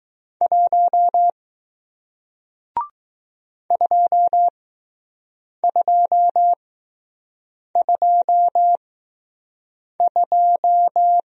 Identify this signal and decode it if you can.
Morse code, heard as a keyed tone, about 20 wpm, spaced as for 6 wpm, 700 Hz. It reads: 1 2222